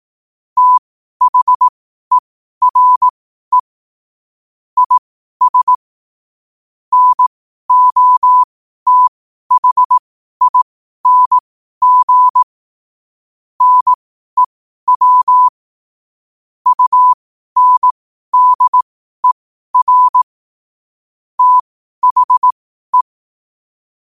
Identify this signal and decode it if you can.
Morse code, heard as a keyed tone, about 18 wpm, spaced as for 12 wpm, 1000 Hz. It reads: THERE IS NOTHING NEW UNDER THE
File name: qotdfiles / 12wpm